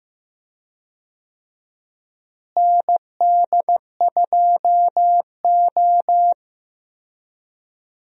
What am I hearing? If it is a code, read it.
Morse code, heard as a keyed tone, about 15 wpm, 700 Hz. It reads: ND2O
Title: USA Call Signs - 4 Characters - Repeated 2x - 15wpm